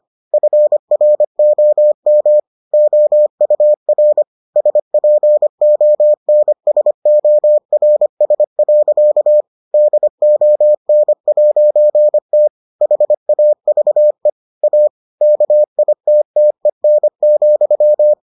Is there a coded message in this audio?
Yes: Morse code from 0.3 to 18.1 s, heard as a keyed tone, about 25 wpm, 600 Hz. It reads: FROM OUR SPONSORS. DON'T HAVE A KITTEN,